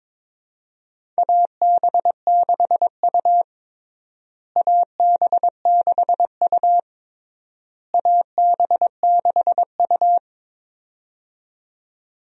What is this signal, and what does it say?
Morse code, heard as a keyed tone, about 22 wpm, 700 Hz. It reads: AB6U AB6U AB6U
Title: USA Call Signs - 4 Characters - Repeated 3x - 22wpm